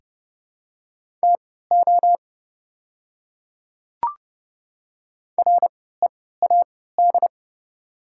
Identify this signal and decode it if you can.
Morse code, heard as a keyed tone, about 30 wpm, spaced as for 17 wpm, 700 Hz. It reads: TO READ
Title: 3x Character Spacing - Top 300 Words - 30wpm